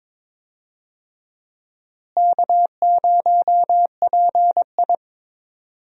Morse code, heard as a keyed tone, about 21 wpm, 700 Hz. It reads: K0PI
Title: USA Call Signs - 4 Characters - Repeated 3x - 22wpm